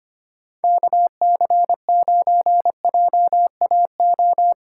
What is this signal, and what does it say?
Morse code, heard as a keyed tone, about 25 wpm, 700 Hz. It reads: KC9JAO
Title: USA Call Signs - 6 Characters - 2x Character Spacing - 25wpm